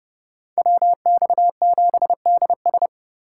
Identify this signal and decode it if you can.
Morse code, heard as a keyed tone, about 30 wpm, 700 Hz. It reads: WX7DS